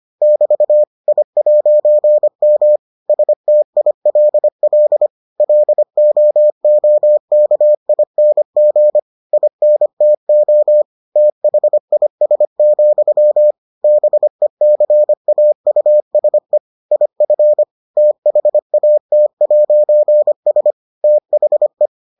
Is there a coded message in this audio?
Yes: Morse code from 0.2 to 21.9 s, heard as a keyed tone, about 25 wpm, 600 Hz. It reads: <BT> I'M STILL LOOKING INTO THIS, BECAUSE IF THAT'S THE